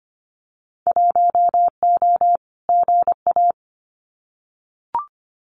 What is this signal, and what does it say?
Morse code, heard as a keyed tone, about 25 wpm, 700 Hz. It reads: 1O GA
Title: Winter Field Day Exchanges 25wpm